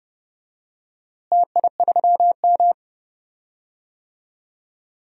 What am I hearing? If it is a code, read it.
Morse code, heard as a keyed tone, about 30 wpm, 700 Hz. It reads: TI3M
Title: DXCC-100 Call Signs - 4 Characters - Repeated 2x - 30wpm